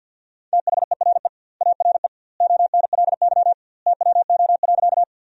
Morse code, heard as a keyed tone, about 50 wpm, 700 Hz. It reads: THERE ARE ONLY TWO.
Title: 2x Word-Spacing - Sentences from Top 200 Words - 50wpm